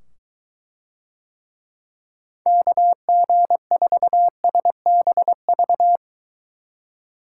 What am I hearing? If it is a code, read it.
Morse code, heard as a keyed tone, about 22 wpm, 700 Hz. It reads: KG4SBV